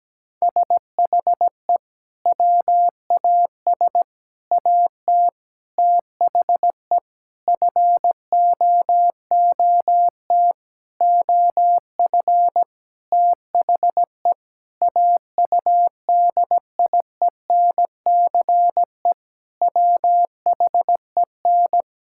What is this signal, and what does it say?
Morse code, heard as a keyed tone, about 17 wpm, 700 Hz. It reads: SHE WAS AT THE FOOT OF THE AUDIENCE WHEN
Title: Sentences from Top 1000 Words - Review Part 5 - 17wpm